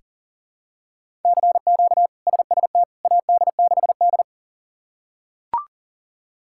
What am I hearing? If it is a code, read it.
Morse code, heard as a keyed tone, about 40 wpm, 700 Hz. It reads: CQ SST AD6D
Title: K1USN SST QSO Practice - 40wpm